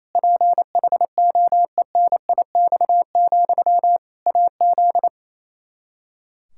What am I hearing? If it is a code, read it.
Morse code, heard as a keyed tone, about 28 wpm, 700 Hz. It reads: PHOENIX, AZ
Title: Practice QSO - Exchange 1 - Beginning - 28wpm